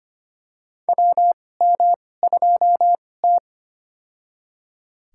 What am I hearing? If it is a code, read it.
Morse code, heard as a keyed tone, about 25 wpm, spaced as for 18 wpm, 700 Hz. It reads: WM2T
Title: USA Call Signs - 4 Characters - 2x Character Spacing - 25wpm